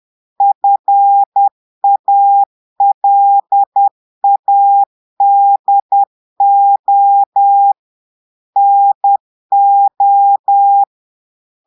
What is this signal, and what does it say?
Morse code, heard as a keyed tone, about 10 wpm, 800 Hz. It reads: FALADO NO
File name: portugais texte en clair 2 vitesse 10 mots minutes